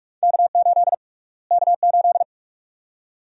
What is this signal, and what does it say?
Morse code, heard as a keyed tone, about 46 wpm, 700 Hz. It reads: K8 K8
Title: USA Call Sign Prefixes - Repeated 2x - 45wpm